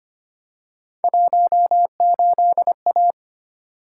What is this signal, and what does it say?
Morse code, heard as a keyed tone, about 25 wpm, 700 Hz. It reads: 18A
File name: Three Letter-Number Combinations - 25wpm